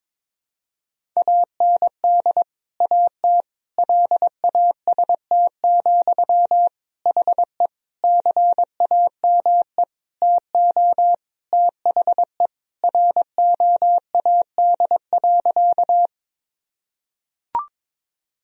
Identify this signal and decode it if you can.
Morse code, heard as a keyed tone, about 21 wpm, 700 Hz. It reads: AND AT LAST, HE CAME TO THE ROAD.
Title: Sentences from Top 1000 Words - Review Part 2 - 22wpm